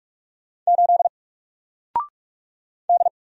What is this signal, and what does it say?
Morse code, heard as a keyed tone, about 46 wpm, 700 Hz. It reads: 8 D
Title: Single Character (Letter:Number:Punctuation:Prosign) Alt 45wpm